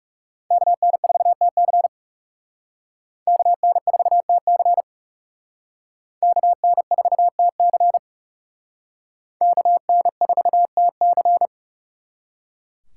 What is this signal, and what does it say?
Morse code, heard as a keyed tone, about 35 wpm, 700 Hz. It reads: KN4TC KN4TC KN4TC KN4TC
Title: Speed Racing - USA 2x2 Call Signs 30wpm